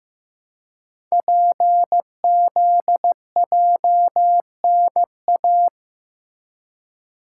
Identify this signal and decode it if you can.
Morse code, heard as a keyed tone, about 15 wpm, 690 Hz. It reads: PZJNA
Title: Code Groups - 5 Letters - 15wpm